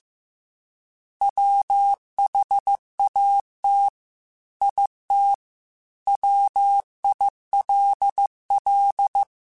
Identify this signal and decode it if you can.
Morse code, heard as a keyed tone, about 15 wpm, 800 Hz. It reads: WHAT IT WILL
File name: Technical Subjects / Technical Subjects-mp3s / OR Gate @13